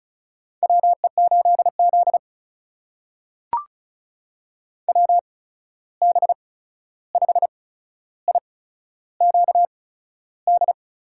Morse code, heard as a keyed tone, about 36 wpm, spaced as for 13 wpm, 700 Hz. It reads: WE8Z WB5IQD
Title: USA Call Signs - 8x Character Spacing - 35wpm